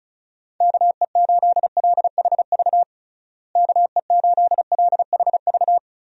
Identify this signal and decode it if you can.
Morse code, heard as a keyed tone, about 34 wpm, 700 Hz. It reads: KE8LHV KE8LHV
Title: USA Call Signs - 6 Characters - Repeated 3x - 35wpm